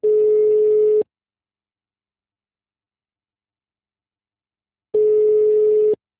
Ringback tone (Europe, 425 Hz).
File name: out- -2013- - - .88357